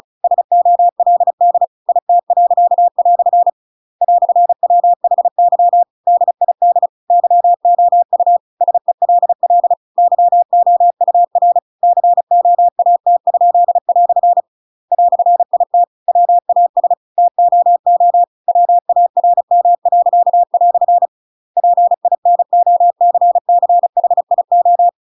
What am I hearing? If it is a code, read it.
Morse code, heard as a keyed tone, about 35 wpm, 700 Hz. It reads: SOLD IT." "WHY DID YOU SELL YOUR COAT?" "IT WAS TOO WARM." PINOCCHIO
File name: Pinocchio / Pinocchio ch8 35 35 ik1hge